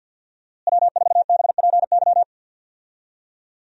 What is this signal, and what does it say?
Morse code, heard as a keyed tone, about 52 wpm, 700 Hz. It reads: W4BPY
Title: USA Call Signs - 5 Characters - Repeated 3x - 50wpm